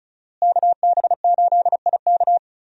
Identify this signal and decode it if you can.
Morse code, heard as a keyed tone, about 34 wpm, 700 Hz. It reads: KB8IK
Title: USA Call Signs - 5 Characters - Repeated 3x - 35wpm